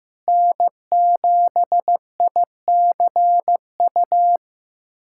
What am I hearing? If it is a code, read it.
Morse code, heard as a keyed tone, about 15 wpm, 700 Hz. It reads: N7ICU
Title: USA Callsign DE Callsign 15wpm